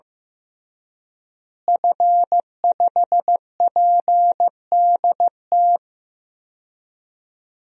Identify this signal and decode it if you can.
Morse code, heard as a keyed tone, about 15 wpm, 700 Hz. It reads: F5PDT
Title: DXCC-100 Call Signs - 5 Characters - Repeated 2x - 15wpm